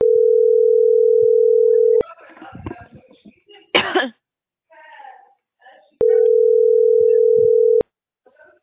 Ringback tone (North America), 440 + 480 Hz.